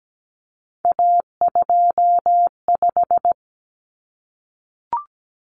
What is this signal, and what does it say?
Morse code, heard as a keyed tone, about 17 wpm, 700 Hz. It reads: A25
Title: Common Cut Numbers - 3-Digit 17wpm